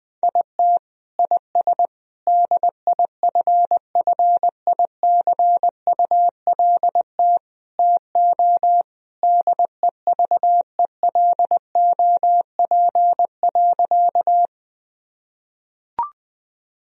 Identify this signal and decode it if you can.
Morse code, heard as a keyed tone, about 20 wpm, 700 Hz. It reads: IT IS DIFFICULT TO DEVELOP.
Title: Sentences from Top 1000 Words - Review Part 2 - 20wpm